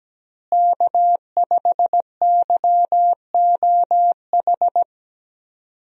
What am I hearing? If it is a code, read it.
Morse code, heard as a keyed tone, about 17 wpm, 700 Hz. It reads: K5YOH
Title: USA Call Signs - 5 Characters - Repeated 2x - 17wpm